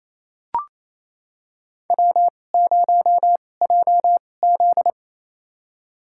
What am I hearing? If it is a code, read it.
Morse code, heard as a keyed tone, about 28 wpm, spaced as for 20 wpm, 700 Hz. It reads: W0JZ
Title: USA Call Signs - 4 Characters - 2x Character Spacing - 28wpm